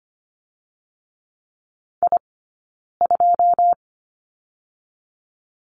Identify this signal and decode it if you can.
Morse code, heard as a keyed tone, about 25 wpm, 700 Hz. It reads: I 2
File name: Lesson 15d - Number 5 - Two Characters 25wpm